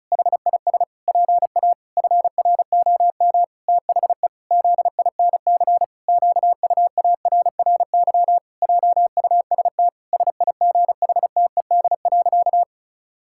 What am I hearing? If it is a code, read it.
Morse code, heard as a keyed tone, about 36 wpm, 700 Hz. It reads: HIS PA FROM THE ZINC QUARRY JUST SIGHTED.
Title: Pangram Challenge v2 - 35wpm